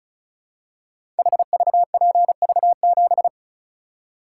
Morse code, heard as a keyed tone, about 36 wpm, 700 Hz. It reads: HVPV7
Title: Code Groups - 5 Letter:Numbers - 35wpm